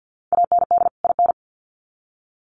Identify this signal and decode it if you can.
Morse code, heard as a keyed tone, about 50 wpm, 700 Hz. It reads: AND IN
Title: Sets of 2 Words 50wpm